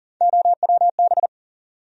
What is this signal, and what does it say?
Morse code, heard as a keyed tone, about 40 wpm, 700 Hz. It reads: OWB
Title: USA Call Sign Postfixes - 3 Letters 40wpm